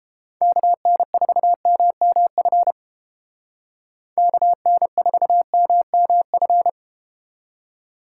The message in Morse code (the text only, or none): KN4MMF KN4MMF